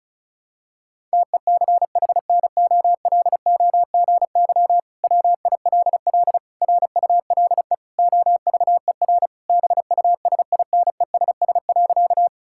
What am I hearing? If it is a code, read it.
Morse code, heard as a keyed tone, about 36 wpm, 700 Hz. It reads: TECHNOLOGY WILL RULE OVER BUSINESS.
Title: Sentences from Top 1000 Words - Review Part 5 - 35wpm